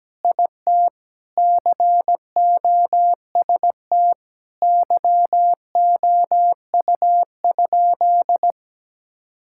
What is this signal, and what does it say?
Morse code, heard as a keyed tone, about 17 wpm, 700 Hz. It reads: IT COST YOU?
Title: Sentences from Top 1000 Words - Review Part 5 - 17wpm